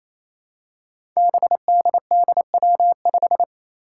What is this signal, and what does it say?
Morse code, heard as a keyed tone, about 28 wpm, 700 Hz. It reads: BDDW5
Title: Five Challenging Letter-Number Combinations 28wpm